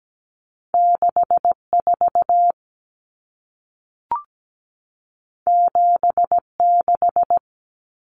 Morse code, heard as a keyed tone, about 17 wpm, 700 Hz. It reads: 64 76
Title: Numbers - 2-Digit 17wpm